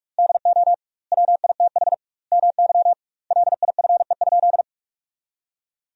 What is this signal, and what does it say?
Morse code, heard as a keyed tone, about 46 wpm, 700 Hz. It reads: DO WITH MY LIFE?
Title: 2x Word-Spacing - Sentences from Top 300 Words - 45wpm